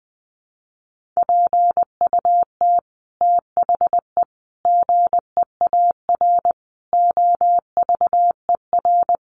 Morse code, heard as a keyed tone, about 20 wpm, 700 Hz. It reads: PUT THE GEAR OVER
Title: Sentences from Top 1400 Words 20wpm